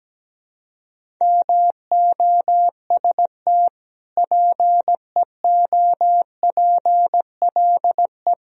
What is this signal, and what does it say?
Morse code, heard as a keyed tone, about 17 wpm, 700 Hz. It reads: MOST PEOPLE
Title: Sentences from Top 1000 Words - Review Part 1 - 17wpm